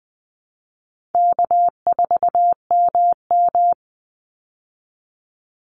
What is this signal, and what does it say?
Morse code, heard as a keyed tone, about 20 wpm, 700 Hz. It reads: K4MM
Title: Easy Call Signs Repeated 2x - 20wpm